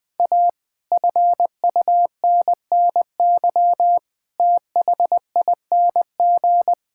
Morse code, heard as a keyed tone, about 21 wpm, 700 Hz. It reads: A FUNNY THING
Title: Sets of 8 Words - 20wpm